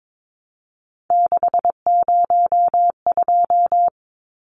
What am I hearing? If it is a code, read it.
Morse code, heard as a keyed tone, about 22 wpm, 700 Hz. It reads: 602
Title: Numbers - 3-Digit 22wpm